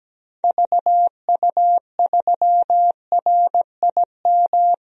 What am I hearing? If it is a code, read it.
Morse code, heard as a keyed tone, about 17 wpm, 700 Hz. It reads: VU3RIM
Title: DXCC-100 Call Signs - 6 Characters - 17wpm